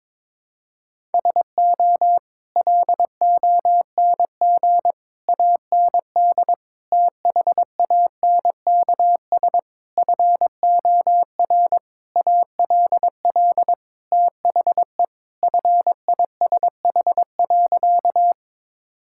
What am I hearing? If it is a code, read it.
Morse code, heard as a keyed tone, about 22 wpm, 700 Hz. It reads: SO LONG AND THANKS FOR ALL THE FISH.
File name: practice / 22wpm / Sentences with 8-Word Phrases - 22wpm